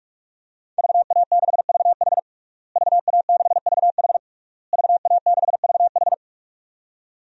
Morse code, heard as a keyed tone, about 46 wpm, 700 Hz. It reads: VA6VH VA6VH VA6VH
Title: DXCC-100 Call Signs - 5 Characters - Repeated 3x - 45wpm